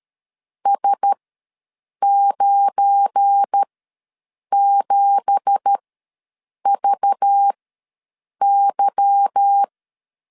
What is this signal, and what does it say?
Morse code, heard as a keyed tone, about 12 wpm, spaced as for 7 wpm, 800 Hz. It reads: S97VY
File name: LEZIONI / 9 2867 ws 12wpm 3431